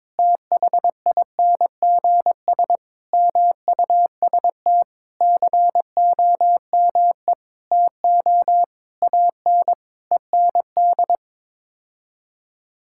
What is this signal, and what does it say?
Morse code, heard as a keyed tone, about 22 wpm, 700 Hz. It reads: THINGS MUST COME TO AN END
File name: practice / 22wpm / Sets of 8 Words - 22wpm